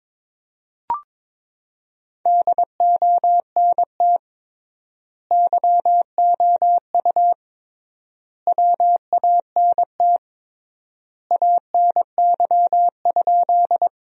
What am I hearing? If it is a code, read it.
Morse code, heard as a keyed tone, about 22 wpm, 700 Hz. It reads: DONT YOU WANT ANY?
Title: 3x Word-Spacing - Sentences from Top 300 Words - 22wpm